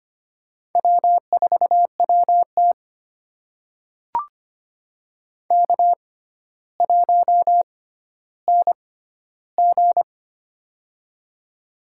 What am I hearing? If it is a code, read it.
Morse code, heard as a keyed tone, about 25 wpm, spaced as for 13 wpm, 700 Hz. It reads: W4WT K1NG
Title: USA Call Signs - 4 Characters - 6x Character Spacing - 25wpm